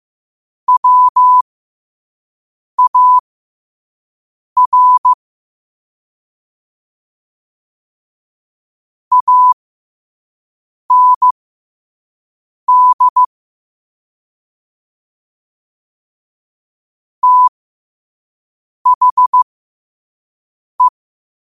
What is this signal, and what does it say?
Morse code, heard as a keyed tone, about 15 wpm, spaced as for 5 wpm, 1000 Hz. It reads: WAR AND THE